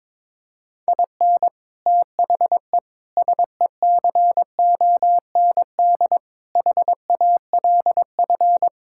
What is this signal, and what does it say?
Morse code, heard as a keyed tone, about 22 wpm, 700 Hz. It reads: IN THE SECOND HALF
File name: practice / 22wpm / Sets of 6 Words - 22wpm